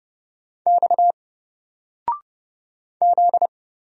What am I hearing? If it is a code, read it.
Morse code, heard as a keyed tone, about 31 wpm, 700 Hz. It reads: X Z
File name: Single-Letter-Number-30wpm